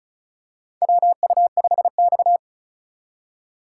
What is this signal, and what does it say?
Morse code, heard as a keyed tone, about 34 wpm, 700 Hz. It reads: WU5X
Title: USA Call Signs - 4 Characters - 3x Character Spacing - 35wpm